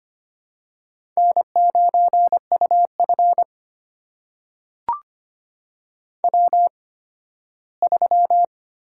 Morse code, heard as a keyed tone, about 25 wpm, spaced as for 15 wpm, 700 Hz. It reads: N9UF W3
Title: USA Call Signs - 8x Character Spacing - 25wpm